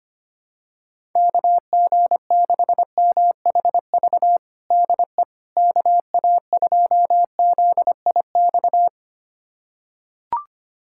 Morse code, heard as a keyed tone, about 25 wpm, 700 Hz. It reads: KG6MHV DE KA2ZIX
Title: USA Callsign DE Callsign 25wpm